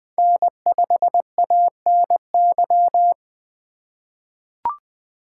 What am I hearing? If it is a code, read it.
Morse code, heard as a keyed tone, about 20 wpm, 700 Hz. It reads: N5ANY
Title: USA Call Signs - 5 Characters - 3x Character Spacing - 20wpm